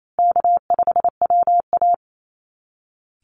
Morse code, heard as a keyed tone, about 28 wpm, 700 Hz. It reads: K5WA
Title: Easy Call Signs Repeated 3x - 28wpm